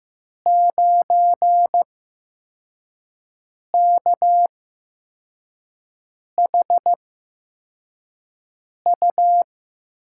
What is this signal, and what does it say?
Morse code, heard as a keyed tone, about 15 wpm, spaced as for 4 wpm, 700 Hz. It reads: 9KHU
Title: USA Call Signs - 6 Characters - 8x Character Spacing - 15wpm